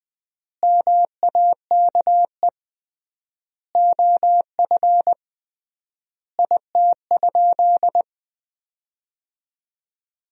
Morse code, heard as a keyed tone, about 20 wpm, 700 Hz. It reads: MAKE OF IT?
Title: 3x Word-Spacing - Sentences from Top 100 Words - 20wpm